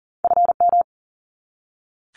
Morse code, heard as a keyed tone, about 39 wpm, 700 Hz. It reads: FM
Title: Abbreviations 40wpm